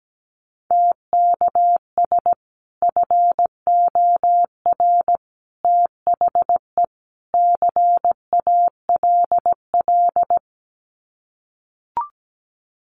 Morse code, heard as a keyed tone, about 17 wpm, 700 Hz. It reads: TKS FOR THE CALL
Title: Common QSO Phrases 17wpm